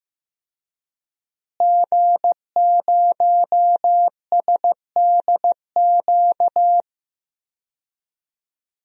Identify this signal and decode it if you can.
Morse code, heard as a keyed tone, about 15 wpm, 700 Hz. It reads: G0SDQ
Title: DXCC-100 Call Signs - 5 Characters - Repeated 3x - 15wpm